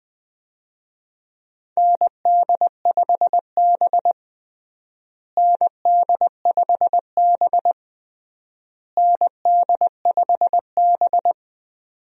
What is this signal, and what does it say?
Morse code, heard as a keyed tone, about 21 wpm, 700 Hz. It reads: ND5B ND5B ND5B
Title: USA Call Signs - 4 Characters - Repeated 3x - 20wpm